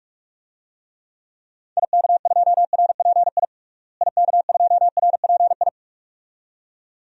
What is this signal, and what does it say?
Morse code, heard as a keyed tone, about 46 wpm, 700 Hz. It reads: IK2RPI IK2RPI
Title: DXCC-100 Call Signs - 6 Characters - Repeated 2x - 45wpm